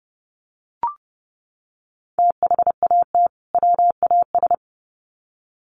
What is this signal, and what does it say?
Morse code, heard as a keyed tone, about 30 wpm, 700 Hz. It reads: THAT WAS
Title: Sets of 2 Words 30wpm